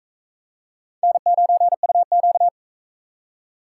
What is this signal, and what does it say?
Morse code, heard as a keyed tone, about 42 wpm, 700 Hz. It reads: N9UQ